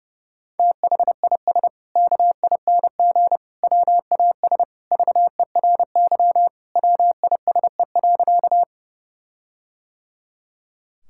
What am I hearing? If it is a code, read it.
Morse code, heard as a keyed tone, about 32 wpm, 700 Hz. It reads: THIS KING WAS VERY WISE.
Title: Sentences from Top 1600 Words 30wpm